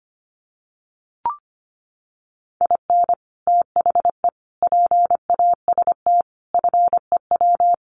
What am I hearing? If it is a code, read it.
Morse code, heard as a keyed tone, about 25 wpm, 700 Hz. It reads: IN THE PAST FEW
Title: Sets of 5 Words 25wpm